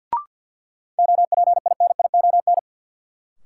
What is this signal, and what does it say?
Morse code, heard as a keyed tone, about 53 wpm, 700 Hz. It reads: OPINION
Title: Sentences from Top 1200 Words - New Word Drill Alt - 50wpm